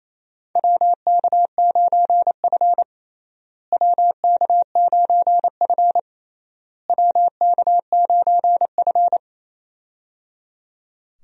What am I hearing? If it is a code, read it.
Morse code, heard as a keyed tone, about 27 wpm, 700 Hz. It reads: WK9F WK9F WK9F
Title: USA Call Signs - 4 Characters - Repeated 3x - 28wpm